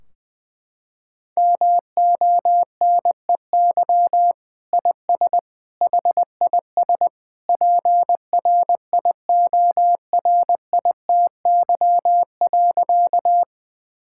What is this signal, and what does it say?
Morse code, heard as a keyed tone, about 20 wpm, 700 Hz. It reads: MONEY IS HIS PRIORITY.
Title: Sentences from Top 1600 Words 20wpm